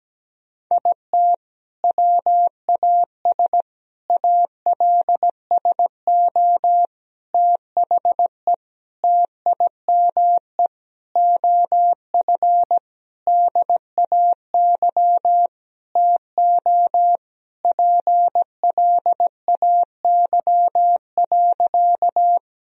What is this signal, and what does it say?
Morse code, heard as a keyed tone, about 17 wpm, 700 Hz. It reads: IT WAS ALSO THE TIME OF DAY TO PLAY.
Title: Sentences from Top 1000 Words - Review Part 1 - 17wpm